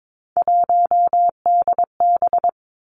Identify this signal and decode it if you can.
Morse code, heard as a keyed tone, about 22 wpm, 700 Hz. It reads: 1DB
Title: Three Challenging Letter-Number Combinations 22wpm